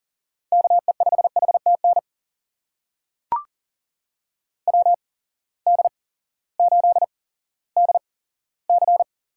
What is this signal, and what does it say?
Morse code, heard as a keyed tone, about 40 wpm, spaced as for 18 wpm, 700 Hz. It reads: KE5HTN WD8DC
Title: USA Call Signs - 6 Characters - 8x Character Spacing - 40wpm